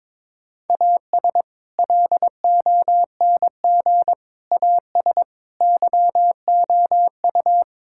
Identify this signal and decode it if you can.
Morse code, heard as a keyed tone, about 22 wpm, 700 Hz. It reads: AS LONG AS YOU
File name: practice / 22wpm / Sentences with 4-Word Phrases - 22wpm